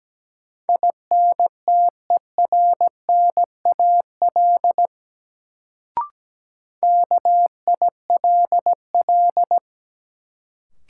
Morse code, heard as a keyed tone, about 17 wpm, 700 Hz. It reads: INTERNAL KILL
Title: Sentences from Top 1400 Words - New Word Drill Alt - 17wpm